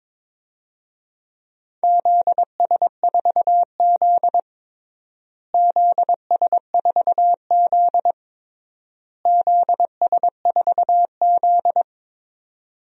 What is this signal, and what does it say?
Morse code, heard as a keyed tone, about 22 wpm, 700 Hz. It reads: ZS4Z ZS4Z ZS4Z
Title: DXCC-100 Call Signs - 4 Characters - Repeated 3x - 22wpm